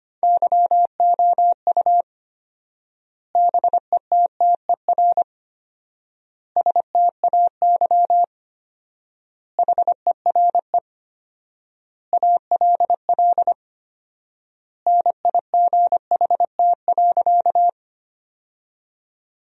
Morse code, heard as a keyed tone, about 25 wpm, 700 Hz. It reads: YOU BETTER STAY HERE ALL NIGHT.
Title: 4x Word-Spacing - Sentences from Top 500 Words - 25wpm